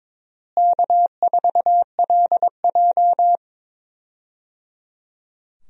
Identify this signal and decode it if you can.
Morse code, heard as a keyed tone, about 23 wpm, 700 Hz. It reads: K4LJ
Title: USA Call Signs - 4 Characters - Repeated 3x - 22wpm